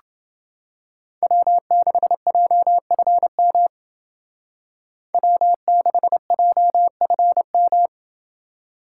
Morse code, heard as a keyed tone, about 30 wpm, 700 Hz. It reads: W6JFM W6JFM